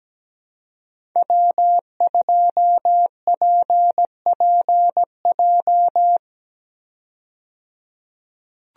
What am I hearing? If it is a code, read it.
Morse code, heard as a keyed tone, about 17 wpm, 700 Hz. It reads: W2PPJ
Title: USA Call Signs - 5 Characters - 17wpm